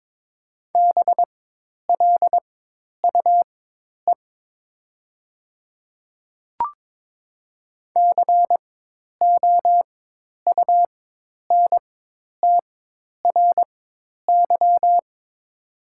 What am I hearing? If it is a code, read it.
Morse code, heard as a keyed tone, about 23 wpm, spaced as for 10 wpm, 700 Hz. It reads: BLUE COUNTRY
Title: 4x Character Spacing - Top 500 Words - 22wpm